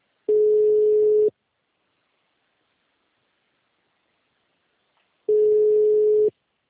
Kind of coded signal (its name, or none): telephone call progress tone